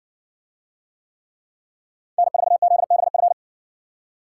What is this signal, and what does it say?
Morse code, heard as a keyed tone, about 60 wpm, 700 Hz. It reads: N4ZBF